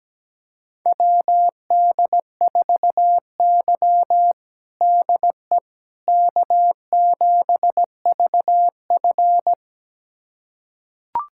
Morse code, heard as a keyed tone, about 17 wpm, 700 Hz. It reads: WD4Y DE K7VF
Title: USA Callsign DE Callsign - 4 Characters - 17wpm